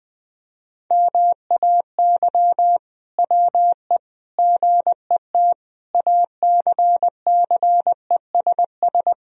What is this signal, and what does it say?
Morse code, heard as a keyed tone, about 20 wpm, 700 Hz. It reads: MAY WE GET ACCESS
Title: Sentences from Top 1000 Words - Review Part 4 - 20wpm